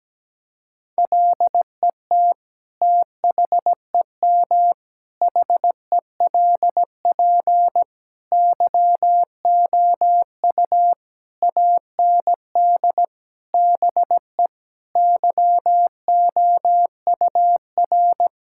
Morse code, heard as a keyed tone, about 17 wpm, 700 Hz. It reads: LET THEM HELP YOU AND BE YOUR
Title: Sentences from Top 1600 Words 17wpm